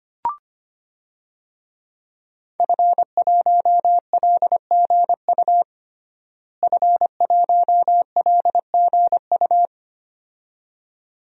Morse code, heard as a keyed tone, about 25 wpm, 700 Hz. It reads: F1LGU F1LGU
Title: DXCC-100 Call Signs - 5 Characters - Repeated 2x - 25wpm